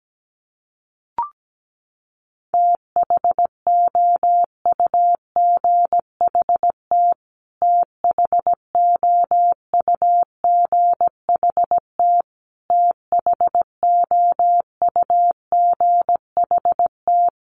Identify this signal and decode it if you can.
Morse code, heard as a keyed tone, about 17 wpm, 700 Hz. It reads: THOUGHT THOUGHT THOUGHT
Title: Top 300 Words Repeated 17wpm